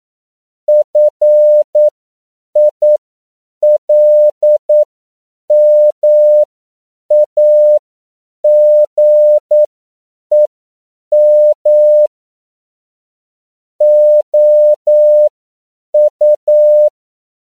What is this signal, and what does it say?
Morse code, heard as a keyed tone, about 9 wpm, spaced as for 7 wpm, 600 Hz. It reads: FILMAGEM OU
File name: CW / 10PPM